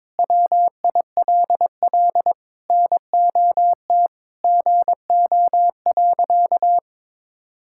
Morse code, heard as a keyed tone, about 21 wpm, 700 Hz. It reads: WILL NOT GO.
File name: practice / 22wpm / Sentences with 4-Word Phrases - 22wpm